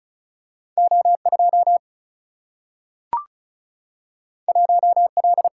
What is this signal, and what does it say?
Morse code, heard as a keyed tone, about 36 wpm, 700 Hz. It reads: O 2 1 L
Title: Single-Letter Single-Number 35wpm